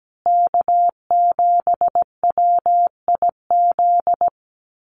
Morse code, heard as a keyed tone, about 17 wpm, 700 Hz. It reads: K7WIZ
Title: USA Call Signs Repeated 2x 17wpm